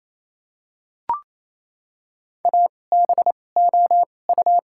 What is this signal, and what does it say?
Morse code, heard as a keyed tone, about 28 wpm, spaced as for 20 wpm, 690 Hz. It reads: ABOU
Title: 2x Character Spacing - Top 500 Words - 28wpm